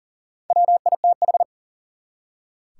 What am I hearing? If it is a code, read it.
Morse code, heard as a keyed tone, about 39 wpm, 700 Hz. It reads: WITH
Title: Top 180 QSO Elements 40wpm